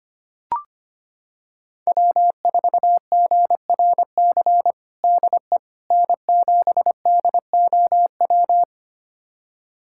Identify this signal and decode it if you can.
Morse code, heard as a keyed tone, about 25 wpm, 700 Hz. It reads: W4GRC DE N7DOW
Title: USA Callsign DE Callsign - 5 Characters - 25wpm